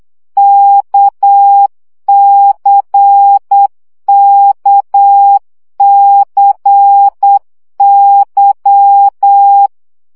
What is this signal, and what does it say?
Morse code, heard as a keyed tone, about 8 wpm, 800 Hz. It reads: KCKCY
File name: CW gyakorlas k c y 7 baud